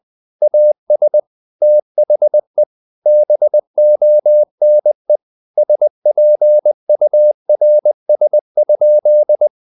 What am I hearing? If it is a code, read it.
Morse code, heard as a keyed tone, about 20 wpm, 600 Hz. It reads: AS THE BONE SPURS?
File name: the old man and the sea part 2 20wpm